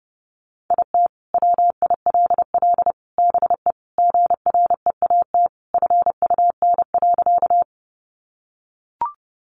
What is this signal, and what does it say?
Morse code, heard as a keyed tone, about 30 wpm, 700 Hz. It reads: IT WILL BE GREAT FUN.
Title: Sentences from Top 700 Words 30wpm